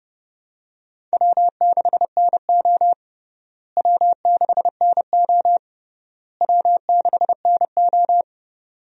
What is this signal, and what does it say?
Morse code, heard as a keyed tone, about 30 wpm, 700 Hz. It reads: W6NO W6NO W6NO
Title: USA Call Signs - 4 Characters - Repeated 3x - 30wpm